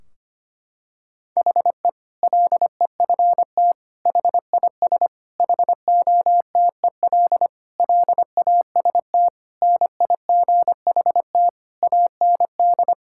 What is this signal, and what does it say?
Morse code, heard as a keyed tone, about 25 wpm, 700 Hz. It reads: HE LEFT HIS HOTEL LAST NIGHT AND
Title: Sentences from Top 1600 Words 25wpm